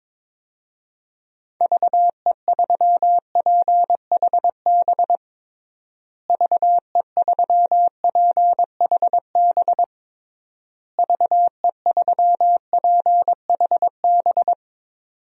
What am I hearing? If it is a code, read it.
Morse code, heard as a keyed tone, about 21 wpm, 700 Hz. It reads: VE3PHB VE3PHB VE3PHB
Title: DXCC-100 Call Signs - 6 Characters - Repeated 3x - 22wpm